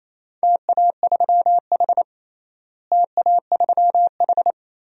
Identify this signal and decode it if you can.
Morse code, heard as a keyed tone, about 28 wpm, 700 Hz. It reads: TA3H TA3H
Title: DXCC-100 Call Signs - 4 Characters - Repeated 3x - 28wpm